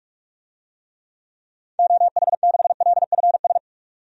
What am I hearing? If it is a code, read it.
Morse code, heard as a keyed tone, about 46 wpm, 700 Hz. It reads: OH6LFS
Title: DXCC-100 Call Signs - 6 Characters - 45wpm